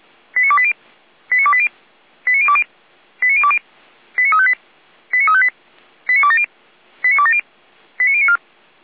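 Five-tone selective call (ZVEI), heard as quick runs of five tones, tones of about 70 ms. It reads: 89280 89280 89020 89020 89358 89358 89270 89270 89094
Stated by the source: Five-tone ZVEI selective call received on an SDR.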